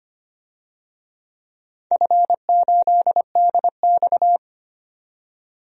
Morse code, heard as a keyed tone, about 25 wpm, 690 Hz. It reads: F8DX
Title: DXCC-100 Call Signs - 4 Characters - 25wpm